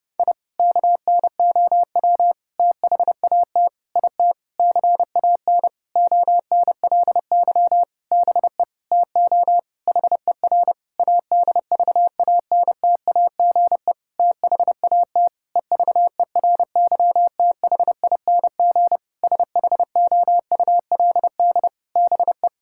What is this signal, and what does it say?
Morse code, heard as a keyed tone, about 30 wpm, 700 Hz. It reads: I KNOW THAT IT CAN ONLY BE TO HER ADVANTAGE THAT EVERYTHING SHOULD BE
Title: Sentences from Top 1000 Words - Review Part 6 - 30wpm